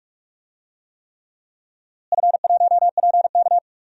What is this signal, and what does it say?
Morse code, heard as a keyed tone, about 42 wpm, 700 Hz. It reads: F1PK
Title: DXCC-100 Call Signs - 4 Characters - Repeated 3x - 45wpm